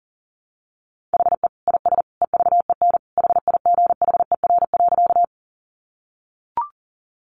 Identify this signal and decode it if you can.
Morse code, heard as a keyed tone, about 40 wpm, 700 Hz. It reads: HE IS EVEN HIGHER.
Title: Additional Easy Phrases 40wpm